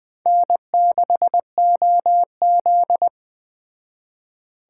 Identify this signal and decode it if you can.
Morse code, heard as a keyed tone, about 20 wpm, 700 Hz. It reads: N6OZ
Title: Speed Racing - USA 1x2 Call Signs 17wpm